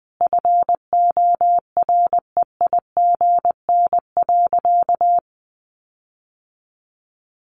Morse code, heard as a keyed tone, about 20 wpm, 700 Hz. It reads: FOREIGN.
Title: Sentences from Top 1400 Words 20wpm